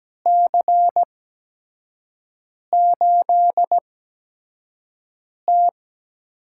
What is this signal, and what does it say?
Morse code, heard as a keyed tone, about 17 wpm, spaced as for 5 wpm, 700 Hz. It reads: C8T
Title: USA Call Signs - 4 Characters - 8x Character Spacing - 17wpm